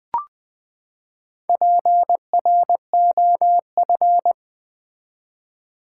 Morse code, heard as a keyed tone, about 20 wpm, 700 Hz. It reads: PROF
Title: Build A Word - 4 Letters At A Time - 12-Letter Words - 20wpm